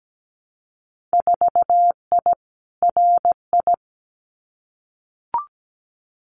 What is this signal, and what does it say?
Morse code, heard as a keyed tone, about 17 wpm, 700 Hz. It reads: 4I RI
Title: Winter Field Day Exchanges Encore 17wpm